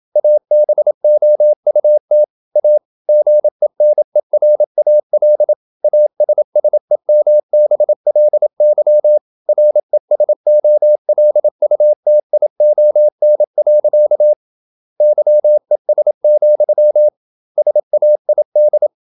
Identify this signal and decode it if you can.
Morse code, heard as a keyed tone, about 27 wpm, 600 Hz. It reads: ABOUT A GENERAL ASSEMBLY RESOLUTION. YES, SAID
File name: Never 0027wpm